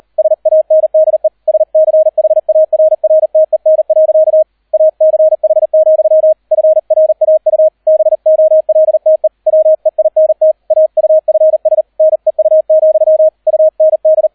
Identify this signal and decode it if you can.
Morse code, heard as a keyed tone, about 38 wpm, 630 Hz. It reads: SANDE SCHARRTEN. ACH, FRAU BOLTE WEINT AUFS NEU, UND